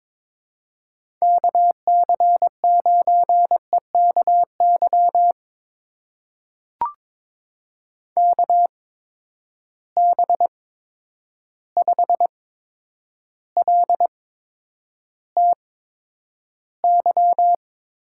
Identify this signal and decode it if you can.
Morse code, heard as a keyed tone, about 22 wpm, spaced as for 9 wpm, 700 Hz. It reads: KC9EKY KB5LTY
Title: USA Call Signs - 6 Characters - 8x Character Spacing - 22wpm